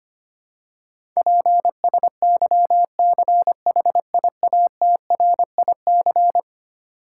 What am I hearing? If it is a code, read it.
Morse code, heard as a keyed tone, about 25 wpm, 700 Hz. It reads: PSYCHIATRIC
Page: morse-code-ninja-practice-25wpm